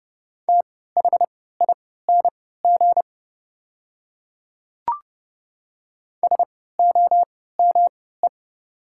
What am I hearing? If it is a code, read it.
Morse code, heard as a keyed tone, about 31 wpm, spaced as for 17 wpm, 700 Hz. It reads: THING SOME
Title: 3x Character Spacing - Top 200 Words - 30wpm